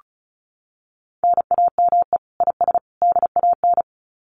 Morse code, heard as a keyed tone, about 34 wpm, 700 Hz. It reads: NAME IS DAN
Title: Common QSO Phrases 35wpm